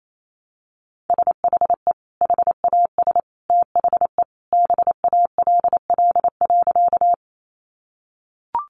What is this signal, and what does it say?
Morse code, heard as a keyed tone, about 28 wpm, 700 Hz. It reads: SHE HAS THE BALL.
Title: Sentences from Top 900 Words 28wpm